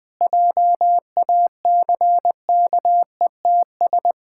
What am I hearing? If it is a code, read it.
Morse code, heard as a keyed tone, about 20 wpm, 700 Hz. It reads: JACKETS
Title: Pangram Challenge v2 - 20wpm